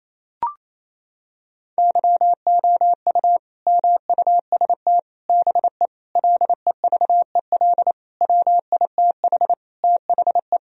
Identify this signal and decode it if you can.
Morse code, heard as a keyed tone, about 28 wpm, 700 Hz. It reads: YOU MUST BE LEVEL WITH THE
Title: Sentences from Top 1000 Words - Review Part 4 - 28wpm